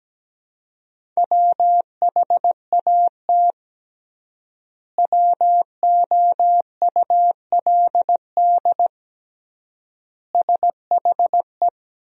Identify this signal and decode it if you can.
Morse code, heard as a keyed tone, about 17 wpm, 700 Hz. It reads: WHAT WOULD SHE
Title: 3x Word-Spacing - Sentences from Top 100 Words - 17wpm